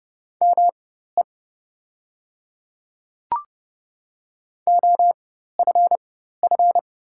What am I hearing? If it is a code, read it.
Morse code, heard as a keyed tone, about 29 wpm, spaced as for 14 wpm, 690 Hz. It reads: ME OFF
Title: 4x Character Spacing - Top 300 Words - 30wpm